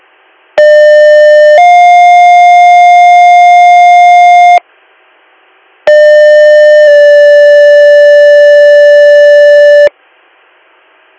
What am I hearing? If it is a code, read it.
Two-tone page, heard as a one-second tone, then a three-second tone, about 600 then 710 Hz, about 600 then 580 Hz. It is heard from 0.6 to 9.9 s.